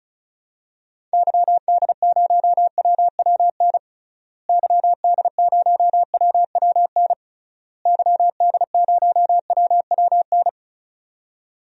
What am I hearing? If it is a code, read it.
Morse code, heard as a keyed tone, about 36 wpm, 700 Hz. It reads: YD0WWN YD0WWN YD0WWN
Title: DXCC-100 Call Signs - 6 Characters - Repeated 3x - 35wpm